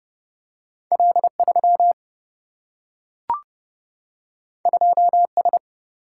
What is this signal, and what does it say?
Morse code, heard as a keyed tone, about 30 wpm, 700 Hz. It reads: L3 2S
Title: Single-Letter Single-Number 30wpm